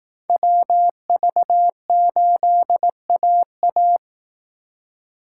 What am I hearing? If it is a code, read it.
Morse code, heard as a keyed tone, about 18 wpm, 700 Hz. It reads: WV8AA
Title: Speed Racing - USA 2x2 Call Signs 15wpm